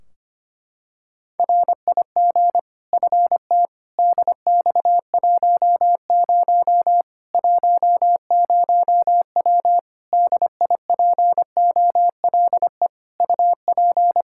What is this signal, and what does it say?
Morse code, heard as a keyed tone, about 25 wpm, 700 Hz. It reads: RIG FT DX10 10W DIPOLE UP
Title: Practice QSO - Exchange 2 - Info - 25wpm